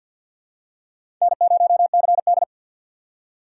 Morse code, heard as a keyed tone, about 51 wpm, 700 Hz. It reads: N0CD
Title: USA Call Signs - 4 Characters - 2x Character Spacing - 50wpm